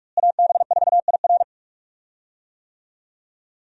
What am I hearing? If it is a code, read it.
Morse code, heard as a keyed tone, about 43 wpm, 700 Hz. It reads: AB4IR
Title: USA Call Signs - 5 Characters - Repeated 2x - 45wpm